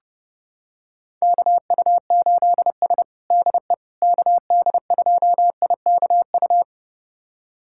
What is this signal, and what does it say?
Morse code, heard as a keyed tone, about 30 wpm, 700 Hz. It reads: KU8S DE KD2IKU
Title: USA Callsign DE Callsign 30wpm